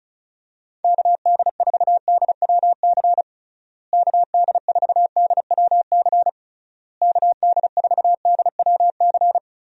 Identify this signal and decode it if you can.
Morse code, heard as a keyed tone, about 36 wpm, 700 Hz. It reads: KD4DWC KD4DWC KD4DWC
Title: USA Call Signs - 6 Characters - Repeated 3x - 35wpm